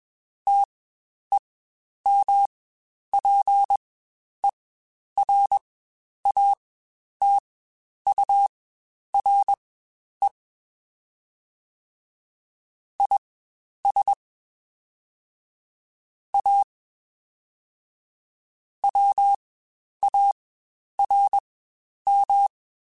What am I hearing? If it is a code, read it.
Morse code, heard as a keyed tone, about 21 wpm, spaced as for 10 wpm, 800 Hz. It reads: TEMPERATURE IS A WARM
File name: Spaced Characters / Spaced Characters-mp3s / Weather 4 10 sp @20